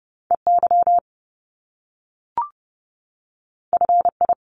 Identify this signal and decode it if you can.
Morse code, heard as a keyed tone, about 30 wpm, 700 Hz. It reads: EY FI
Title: Top Two Letter Combinations in Words 30wpm